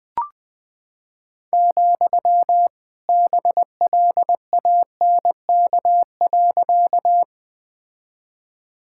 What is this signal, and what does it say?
Morse code, heard as a keyed tone, about 20 wpm, 700 Hz. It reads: , BLANK.
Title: Sentences with 2-Word Phrases - Spoken Beginning - 20wpm